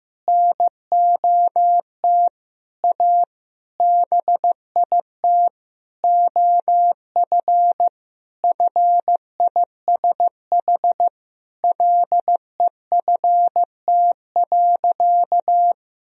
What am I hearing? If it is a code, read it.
Morse code, heard as a keyed tone, about 15 wpm, 700 Hz. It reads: NOT A BIT OF FISH LEFT.
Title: Sentences from Top 1000 Words - Review Part 1 - 15wpm